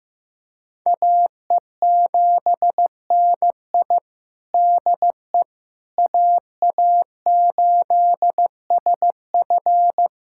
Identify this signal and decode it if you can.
Morse code, heard as a keyed tone, about 15 wpm, 700 Hz. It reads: AE7NI DE AA8SF
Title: USA Callsign DE Callsign - 5 Characters - 15wpm